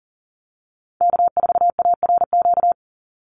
Morse code, heard as a keyed tone, about 40 wpm, 700 Hz. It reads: K4ARQ
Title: Call Signs Repeated 3x - 40wpm